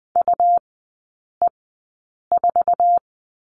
Morse code, heard as a keyed tone, about 19 wpm, spaced as for 8 wpm, 700 Hz. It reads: UE4
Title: 100 Lesson 17e - Letter u - Three Characters 20wpm